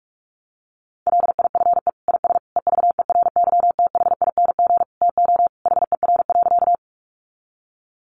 Morse code, heard as a keyed tone, about 44 wpm, 700 Hz. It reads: LIFE IS EVERYTHING TO HER.
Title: Sentences from Top 700 Words 45wpm